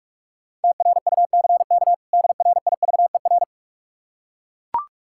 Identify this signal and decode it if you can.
Morse code, heard as a keyed tone, about 44 wpm, 700 Hz. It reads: TRUCK DRIVER
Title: Practice QSO - Exchange 3 - More Info - 45wpm